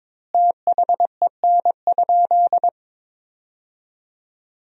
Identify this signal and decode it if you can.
Morse code, heard as a keyed tone, about 23 wpm, 700 Hz. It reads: THEN?
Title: 3x Word-Spacing - Sentences from Top 100 Words - 22wpm